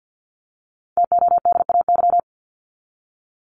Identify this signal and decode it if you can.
Morse code, heard as a keyed tone, about 50 wpm, 700 Hz. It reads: TODAY
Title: Top 501-750 Words 50wpm